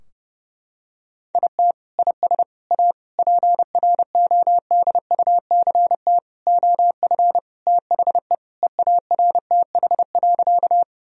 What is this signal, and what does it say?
Morse code, heard as a keyed tone, about 30 wpm, 700 Hz. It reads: IT IS A PRODUCT OF THE EARTH.
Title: Sentences from Top 1000 Words - Review Part 2 - 30wpm